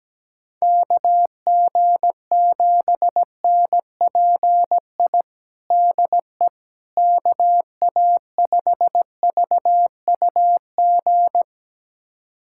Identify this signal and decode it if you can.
Morse code, heard as a keyed tone, about 17 wpm, 700 Hz. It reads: KG7NPI DE KA5VUG
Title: USA Callsign DE Callsign - 6 Characters - 17wpm